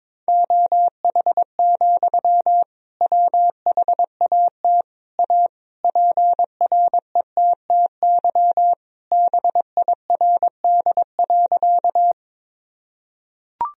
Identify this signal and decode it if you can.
Morse code, heard as a keyed tone, about 22 wpm, 700 Hz. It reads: OH, WHAT A PRETTY BIRD.
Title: Sentences from Top 1000 Words - Review Part 2 - 22wpm